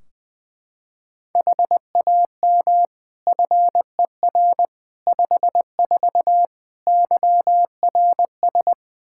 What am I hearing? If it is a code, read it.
Morse code, heard as a keyed tone, about 20 wpm, 700 Hz. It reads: HAM FER 54 YRS
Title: Practice QSO - Exchange 3 - More Info - 20wpm